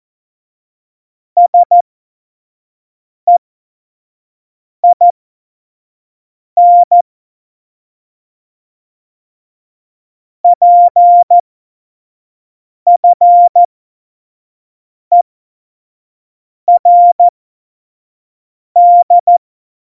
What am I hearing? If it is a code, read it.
Morse code, heard as a keyed tone, about 14 wpm, spaced as for 5 wpm, 700 Hz. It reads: SEIN PFERD
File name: Morsekurs / Witze / Witz 09. April 2019, 25 F70